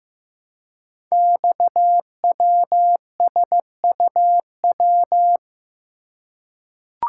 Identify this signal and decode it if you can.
Morse code, heard as a keyed tone, about 15 wpm, 700 Hz. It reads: XWSUW
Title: Code Groups - 5 Letters - 15wpm